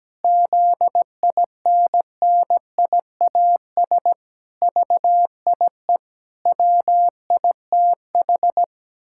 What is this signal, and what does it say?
Morse code, heard as a keyed tone, about 17 wpm, 700 Hz. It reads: ZINNIAS VIE WITH
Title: Pangram Challenge v2 - 17wpm